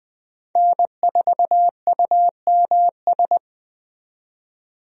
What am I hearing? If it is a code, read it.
Morse code, heard as a keyed tone, about 20 wpm, 700 Hz. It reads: N4UMS
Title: USA Call Signs - 5 Characters - 20wpm